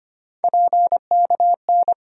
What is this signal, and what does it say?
Morse code, heard as a keyed tone, about 25 wpm, 700 Hz. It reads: PKN
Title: USA Call Sign Postfixes - 3 Letters 25wpm